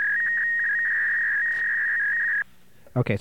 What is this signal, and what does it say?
Fax machine handshake, with speech.